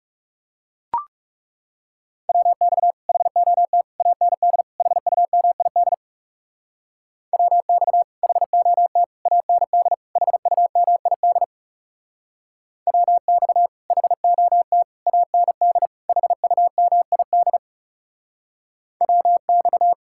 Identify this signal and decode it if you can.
Morse code, heard as a keyed tone, about 40 wpm, 700 Hz. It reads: WX HOT AND HUMID WX HOT AND HUMID WX HOT AND HUMID WX